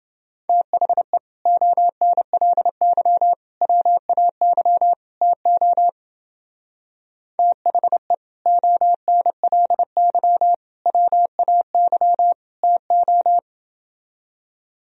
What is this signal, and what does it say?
Morse code, heard as a keyed tone, about 29 wpm, 700 Hz. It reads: THE ONLY WAY TO THE ONLY WAY TO